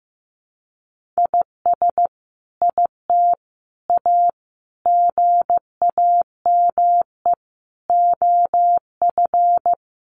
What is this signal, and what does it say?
Morse code, heard as a keyed tone, about 15 wpm, 700 Hz. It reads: IS IT A GAME OF
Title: Sentences from Top 1200 Words 15wpm